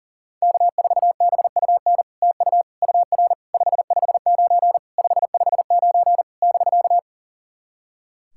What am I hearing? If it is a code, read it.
Morse code, heard as a keyed tone, about 37 wpm, 700 Hz. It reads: K4BUN TU UR 559 559 <BK>
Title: SOTA QSO Practice 40wpm